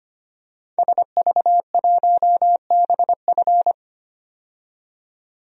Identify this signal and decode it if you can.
Morse code, heard as a keyed tone, about 25 wpm, 700 Hz. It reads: SV1BF
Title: DXCC-100 Call Signs - 5 Characters - 25wpm